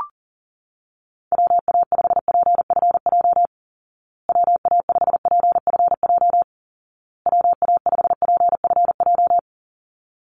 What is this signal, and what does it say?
Morse code, heard as a keyed tone, about 40 wpm, 700 Hz. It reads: WA5PFJ WA5PFJ WA5PFJ
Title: Call Signs Repeated 2x - 40wpm